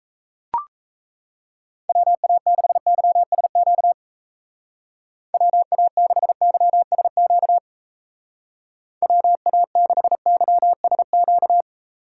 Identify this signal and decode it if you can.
Morse code, heard as a keyed tone, about 34 wpm, 700 Hz. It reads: WA6YSQ WA6YSQ WA6YSQ
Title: Speed Racing - USA 2x3 Call Signs 28wpm